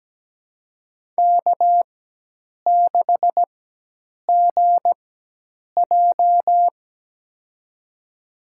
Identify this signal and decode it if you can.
Morse code, heard as a keyed tone, about 17 wpm, spaced as for 8 wpm, 700 Hz. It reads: K6GJ
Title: USA Call Signs - 4 Characters - 4x Character Spacing - 17wpm